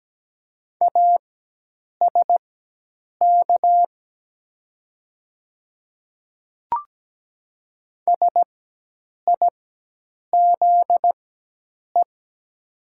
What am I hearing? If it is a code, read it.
Morse code, heard as a keyed tone, about 17 wpm, spaced as for 8 wpm, 700 Hz. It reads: ASK SIZE
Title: 4x Character Spacing - Top 500 Words - 17wpm